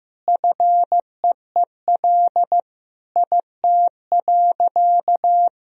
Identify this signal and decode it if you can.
Morse code, heard as a keyed tone, about 15 wpm, 700 Hz. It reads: FEEL IT.
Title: Sentences from Top 400 Words and Abbreviations 15wpm